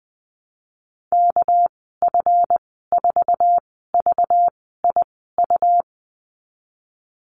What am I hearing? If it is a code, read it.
Morse code, heard as a keyed tone, about 20 wpm, spaced as for 15 wpm, 700 Hz. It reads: KF4VIU
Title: 238 Lesson 38g - Letter q - Call Signs 20wpm